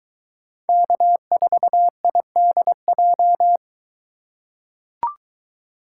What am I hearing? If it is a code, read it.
Morse code, heard as a keyed tone, about 22 wpm, 700 Hz. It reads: K4IDJ